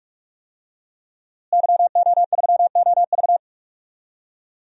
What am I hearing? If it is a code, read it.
Morse code, heard as a keyed tone, about 46 wpm, 700 Hz. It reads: YO3OV
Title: DXCC-100 Call Signs - 5 Characters - Repeated 3x - 45wpm